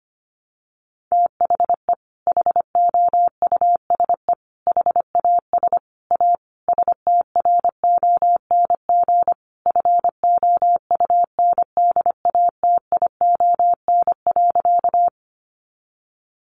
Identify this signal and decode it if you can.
Morse code, heard as a keyed tone, about 25 wpm, 700 Hz. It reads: THE HOUSE HAS A STRONG FOUNDATION.
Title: Sentences from Top 1200 Words 25wpm